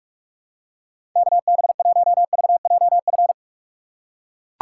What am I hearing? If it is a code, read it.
Morse code, heard as a keyed tone, about 41 wpm, 700 Hz. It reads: KB1VJF
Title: USA Call Signs - 6 Characters - 2x Character Spacing - 45wpm